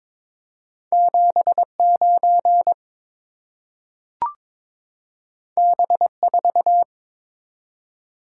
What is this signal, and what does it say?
Morse code, heard as a keyed tone, about 21 wpm, 700 Hz. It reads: 79 B4
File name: practice / 22wpm / Two Letter-Number Combinations - 22wpm